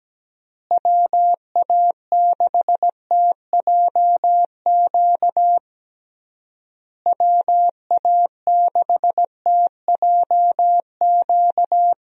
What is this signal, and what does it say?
Morse code, heard as a keyed tone, about 17 wpm, 700 Hz. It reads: WA6TJQ WA6TJQ
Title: USA Call Signs - 6 Characters - Repeated 3x - 17wpm